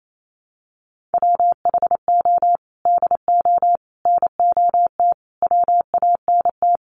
Morse code, heard as a keyed tone, about 28 wpm, 700 Hz. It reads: WHO DO NOT WANT
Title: Sets of 5 Words Encore2 28wpm